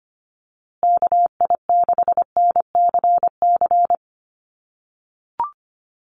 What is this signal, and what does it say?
Morse code, heard as a keyed tone, about 25 wpm, 700 Hz. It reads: KI6NCC
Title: USA Call Signs Repeated 3x 25wpm